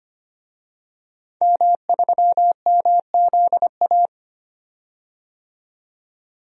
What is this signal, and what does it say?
Morse code, heard as a keyed tone, about 25 wpm, 700 Hz. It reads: M3MZA
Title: DXCC-100 Call Signs - 5 Characters - 25wpm